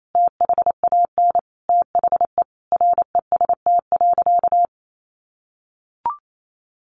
Morse code, heard as a keyed tone, about 28 wpm, 700 Hz. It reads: THAN THE REST.
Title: Sentences from Top 1400 Words 28wpm